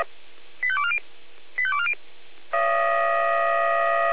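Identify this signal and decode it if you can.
Five-tone selective call (ZVEI), heard as quick runs of five tones, tones of about 70 ms. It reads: 85240 85240
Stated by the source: ZVEI five-tone selective call received on SDR.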